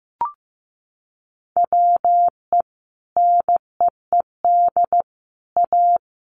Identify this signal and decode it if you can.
Morse code, heard as a keyed tone, about 15 wpm, 700 Hz. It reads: WE NEED A
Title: Sentences from Top 1500 Words 15wpm